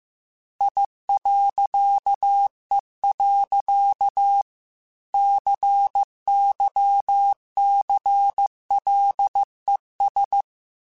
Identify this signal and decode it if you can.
Morse code, heard as a keyed tone, about 15 wpm, 800 Hz. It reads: I.E. CYCLES
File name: Technical Subjects / Technical Subjects-mp3s / Reactance @13